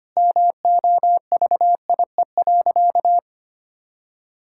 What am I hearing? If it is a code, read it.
Morse code, heard as a keyed tone, about 25 wpm, 700 Hz. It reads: MOVIE.
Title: Sentences from Top 1000 Words - Review Part 5 - 25wpm